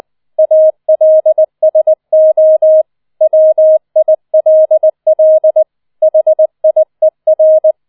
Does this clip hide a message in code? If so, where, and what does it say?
Yes: Morse code from 0.4 to 7.7 s, heard as a keyed tone, about 20 wpm, 630 Hz. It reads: ALSO WILL HIER